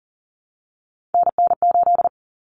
Morse code, heard as a keyed tone, about 40 wpm, 700 Hz. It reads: NN8
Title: Common Cut Numbers - 3-Digit 40wpm